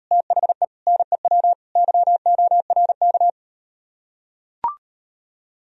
Morse code, heard as a keyed tone, about 38 wpm, 700 Hz. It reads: THE NEW YORK